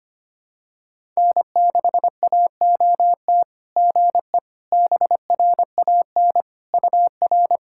Morse code, heard as a keyed tone, about 25 wpm, 690 Hz. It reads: N6AOT GE BRAN UR
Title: SOTA QSO Practice 25wpm